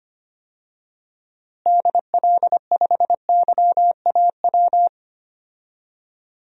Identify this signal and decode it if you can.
Morse code, heard as a keyed tone, about 25 wpm, 700 Hz. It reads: DL5YAW
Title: DXCC-100 Call Signs - 6 Characters - Repeated 3x - 25wpm